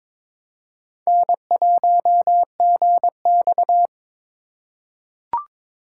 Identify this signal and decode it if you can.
Morse code, heard as a keyed tone, about 22 wpm, 700 Hz. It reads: N1GX
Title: USA Call Signs - 4 Characters - 3x Character Spacing - 22wpm